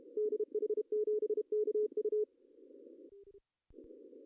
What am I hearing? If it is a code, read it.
Morse code, heard as a keyed tone, about 32 wpm, 420 Hz. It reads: DH7KU
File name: CW / cqww-cw-k5tr